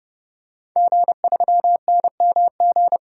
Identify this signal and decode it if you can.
Morse code, heard as a keyed tone, about 30 wpm, 700 Hz. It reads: G3NMG
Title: DXCC-100 Call Signs - 5 Characters - 30wpm